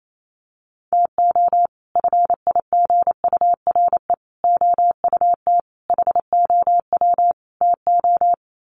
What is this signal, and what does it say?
Morse code, heard as a keyed tone, about 28 wpm, 700 Hz. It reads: TO FIGURE OUT HOW TO
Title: Sets of 5 Words 28wpm